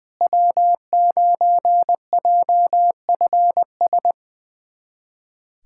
Morse code, heard as a keyed tone, about 21 wpm, 700 Hz. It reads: W9JFS
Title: USA Callsign DE Callsign - 5 Characters - 20wpm